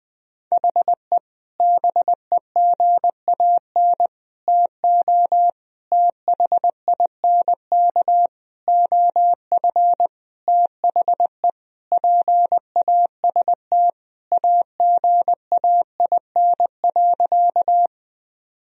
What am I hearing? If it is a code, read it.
Morse code, heard as a keyed tone, about 20 wpm, 700 Hz. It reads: HE BEGAN TO THINK OF THE PAST AGAIN.
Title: Sentences from Top 1000 Words - Review Part 3 - 20wpm